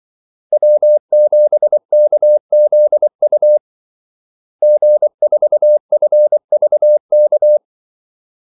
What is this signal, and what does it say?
Morse code, heard as a keyed tone, about 24 wpm, 600 Hz. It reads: W7KZU G4FVK
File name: Callsigns-24wpm